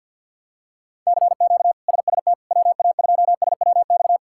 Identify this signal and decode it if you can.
Morse code, heard as a keyed tone, about 53 wpm, 700 Hz. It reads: CQ SST WA2SWX
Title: K1USN SST QSO Practice - 50wpm